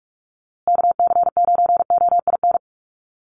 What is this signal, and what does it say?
Morse code, heard as a keyed tone, about 44 wpm, 700 Hz. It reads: KC9OIN
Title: USA Call Signs Repeated 2x 45wpm